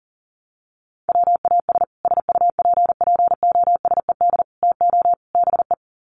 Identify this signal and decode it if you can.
Morse code, heard as a keyed tone, about 40 wpm, 700 Hz. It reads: WAS SUPPOSED TO BE
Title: Sets of 4 Words 40wpm